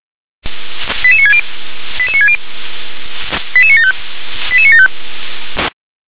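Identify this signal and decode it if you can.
Five-tone selective call (ZVEI), heard as quick runs of five tones, tones of about 70 ms. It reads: 88960 88960 88975 88975
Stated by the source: ZVEI five-tone selective call received by SDR, code 88966.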